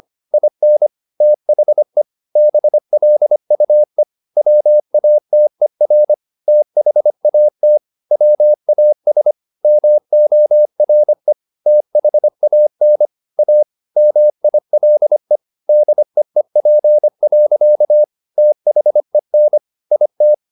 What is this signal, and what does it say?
Morse code, heard as a keyed tone, about 25 wpm, 600 Hz. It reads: IN THE BLUE WATER THAT WAS MORE THAN A MILE DEEP. THEN IT